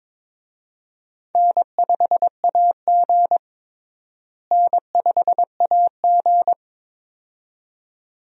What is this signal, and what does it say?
Morse code, heard as a keyed tone, about 21 wpm, 700 Hz. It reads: N5AG N5AG
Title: USA Call Signs - 4 Characters - Repeated 2x - 22wpm